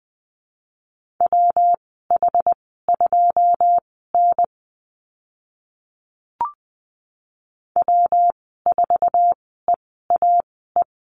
Morse code, heard as a keyed tone, about 20 wpm, spaced as for 14 wpm, 700 Hz. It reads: WH2N W4EAE
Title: 122 Lesson 20g - Number 3 - Call Signs 20wpm